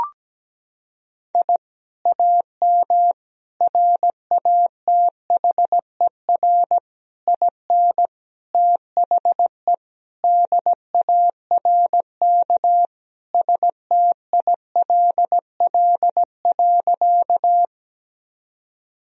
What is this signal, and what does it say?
Morse code, heard as a keyed tone, about 17 wpm, 700 Hz. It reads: I AM RATHER IN THE DARK STILL.
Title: Sentences from Top 1000 Words - Review Part 4 - 17wpm